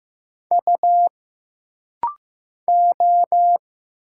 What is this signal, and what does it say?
Morse code, heard as a keyed tone, about 15 wpm, 700 Hz. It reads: U O
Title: Single Letters - Rapid-Fire 15wpm